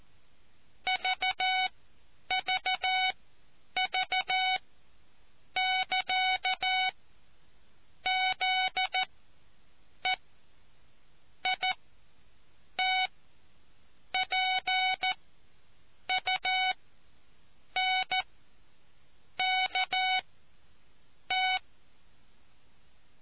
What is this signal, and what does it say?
Morse code, heard as a keyed tone, about 14 wpm, spaced as for 6 wpm, 750 Hz. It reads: VVV<KA>ZEITPUNKT